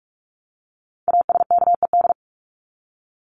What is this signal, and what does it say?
Morse code, heard as a keyed tone, about 44 wpm, 700 Hz. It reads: ASKED
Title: ED Suffix - Top 100 Words with Base Words 45wpm